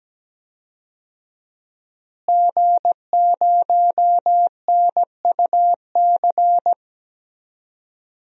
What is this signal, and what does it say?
Morse code, heard as a keyed tone, about 17 wpm, 700 Hz. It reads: G0NUC
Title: DXCC-100 Call Signs - 5 Characters - Repeated 2x - 17wpm